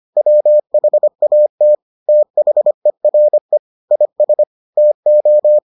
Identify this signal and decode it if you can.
Morse code, heard as a keyed tone, about 25 wpm, 600 Hz. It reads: WHAT THERE IS TO